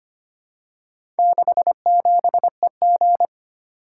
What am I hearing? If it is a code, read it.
Morse code, heard as a keyed tone, about 25 wpm, 700 Hz. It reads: 67EG
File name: practice / 25wpm / Four Letter-Number Combinations - 25wpm